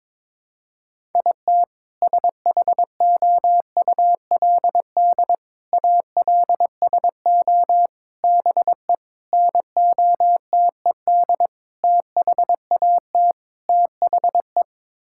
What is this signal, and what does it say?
Morse code, heard as a keyed tone, about 22 wpm, 700 Hz. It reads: IT SHOULD ALSO BE NOTED THAT THE
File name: practice / 22wpm / Sets of 7 Words - 22wpm